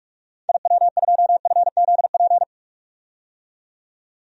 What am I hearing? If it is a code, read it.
Morse code, heard as a keyed tone, about 41 wpm, 700 Hz. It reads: IW2FZP
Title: DXCC-100 Call Signs - 6 Characters - 45wpm